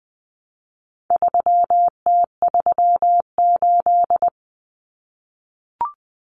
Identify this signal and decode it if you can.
Morse code, heard as a keyed tone, about 20 wpm, 700 Hz. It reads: 3T38
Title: Common Cut Numbers - 4-Digit 20wpm